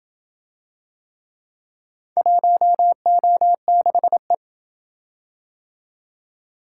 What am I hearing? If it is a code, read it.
Morse code, heard as a keyed tone, about 27 wpm, 700 Hz. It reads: 1O6E